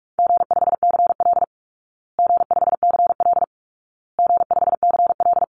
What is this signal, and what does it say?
Morse code, heard as a keyed tone, about 44 wpm, 700 Hz. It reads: G5CL G5CL G5CL
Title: Call Signs Repeated 2x - 45wpm